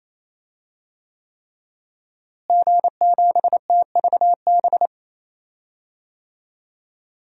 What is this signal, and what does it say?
Morse code, heard as a keyed tone, about 28 wpm, 700 Hz. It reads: G7TVB
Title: DXCC-100 Call Signs - 5 Characters - Repeated 3x - 28wpm